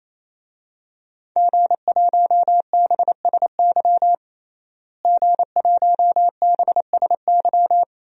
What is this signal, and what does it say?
Morse code, heard as a keyed tone, about 28 wpm, 700 Hz. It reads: G1BSY G1BSY
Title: DXCC-100 Call Signs - 5 Characters - Repeated 3x - 28wpm